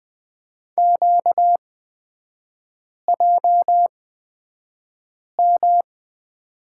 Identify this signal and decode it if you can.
Morse code, heard as a keyed tone, about 20 wpm, spaced as for 5 wpm, 700 Hz. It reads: QJM
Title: 236 Lesson 38e - Letter q - Three Characters 20wpm